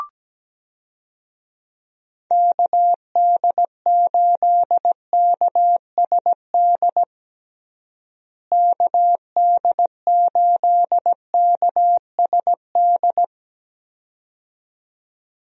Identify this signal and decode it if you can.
Morse code, heard as a keyed tone, about 17 wpm, 700 Hz. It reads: KD8KSD KD8KSD
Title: USA Call Signs - 6 Characters - Repeated 2x - 17wpm